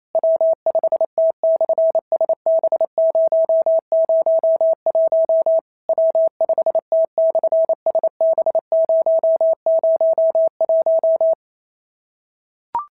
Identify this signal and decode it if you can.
Morse code, heard as a keyed tone, about 28 wpm, 650 Hz. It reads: W5T/SB001 W5T/SB001
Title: SOTA QSO Practice 28wpm